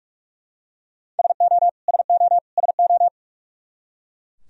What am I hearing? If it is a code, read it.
Morse code, heard as a keyed tone, about 41 wpm, 700 Hz. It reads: SO SO SO
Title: Top 100 Words Repeated 45wpm